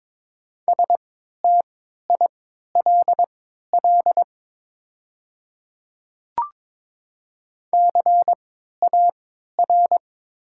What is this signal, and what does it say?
Morse code, heard as a keyed tone, about 22 wpm, spaced as for 13 wpm, 700 Hz. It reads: STILL CAR
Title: 3x Character Spacing - Top 300 Words - 22wpm